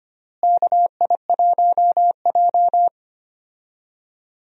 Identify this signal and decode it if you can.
Morse code, heard as a keyed tone, about 25 wpm, 700 Hz. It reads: KI1J
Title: USA Call Signs - 4 Characters - Repeated 2x - 25wpm